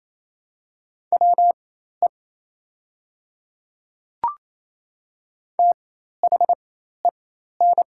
Morse code, heard as a keyed tone, about 28 wpm, spaced as for 13 wpm, 700 Hz. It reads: WE THEN
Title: 4x Character Spacing - Top 300 Words - 28wpm